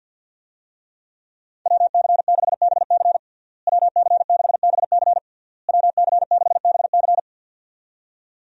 Morse code, heard as a keyed tone, about 46 wpm, 700 Hz. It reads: WC6BC WC6BC WC6BC
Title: USA Call Signs - 5 Characters - Repeated 3x - 50wpm